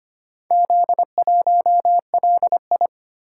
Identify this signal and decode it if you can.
Morse code, heard as a keyed tone, about 25 wpm, 700 Hz. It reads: Z1LI
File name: practice / 25wpm / Four Letter-Number Combinations - 25wpm